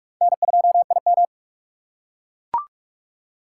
Morse code, heard as a keyed tone, about 44 wpm, 700 Hz. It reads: N2IM
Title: USA Call Signs - 4 Characters - 6x Character Spacing - 45wpm